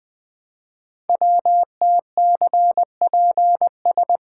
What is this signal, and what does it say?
Morse code, heard as a keyed tone, about 20 wpm, 700 Hz. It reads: WTCPS
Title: Code Groups - 5 Letters - 20wpm